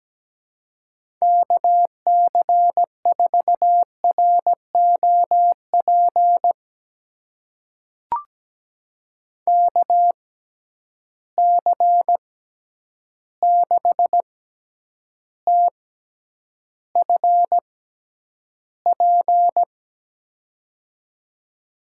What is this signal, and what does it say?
Morse code, heard as a keyed tone, about 17 wpm, spaced as for 9 wpm, 700 Hz. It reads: KC4ROP KC6TFP
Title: USA Call Signs - 6 Characters - 6x Character Spacing - 17wpm